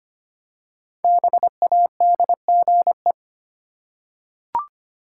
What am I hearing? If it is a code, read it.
Morse code, heard as a keyed tone, about 25 wpm, 700 Hz. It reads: BADGE
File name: 5 Letter Words - First 3 Letters Given - 25wpm